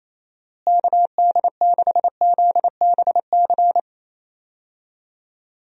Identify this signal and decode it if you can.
Morse code, heard as a keyed tone, about 28 wpm, 700 Hz. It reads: KD6ZBC
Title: USA Call Signs - 6 Characters - Repeated 3x - 28wpm